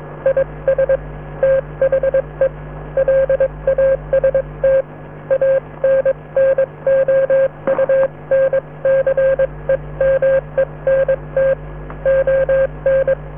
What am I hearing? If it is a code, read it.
Morse code, heard as a keyed tone, about 22 wpm, spaced as for 19 wpm, 540 Hz. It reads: IS THE LAST ANNOUNCEMENT ON